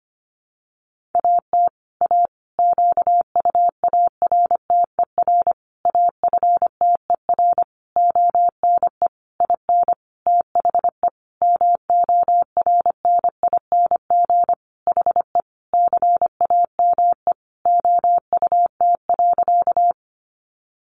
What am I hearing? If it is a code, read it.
Morse code, heard as a keyed tone, about 25 wpm, 700 Hz. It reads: AT A QUARTER AFTER ONE IN THE MORNING HE CAME OUT.
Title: Sentences from Top 1300 Words 25wpm